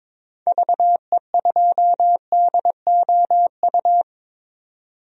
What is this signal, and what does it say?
Morse code, heard as a keyed tone, about 22 wpm, 700 Hz. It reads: VE2DOU
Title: DXCC-100 Call Signs - 6 Characters - 22wpm